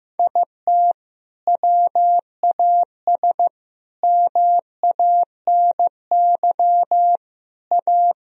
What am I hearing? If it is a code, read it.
Morse code, heard as a keyed tone, about 15 wpm, 700 Hz. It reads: IT WAS MANY A
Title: Sentences from Top 1000 Words - Review Part 1 - 15wpm